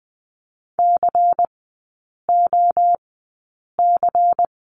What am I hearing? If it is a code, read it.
Morse code, heard as a keyed tone, about 20 wpm, spaced as for 8 wpm, 700 Hz. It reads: COC
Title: 093 Lesson 16e - Letter c - Three Characters 20wpm